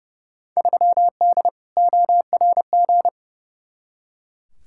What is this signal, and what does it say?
Morse code, heard as a keyed tone, about 31 wpm, 700 Hz. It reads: 3D ORG
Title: Field Day Exchanges Encore - 30wpm